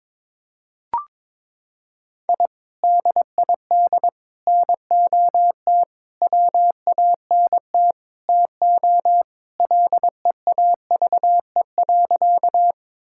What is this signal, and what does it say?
Morse code, heard as a keyed tone, about 22 wpm, 700 Hz. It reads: I DID NOT WANT TO LEAVE.
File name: practice / 22wpm / Sentences with 4-Word Phrases - 22wpm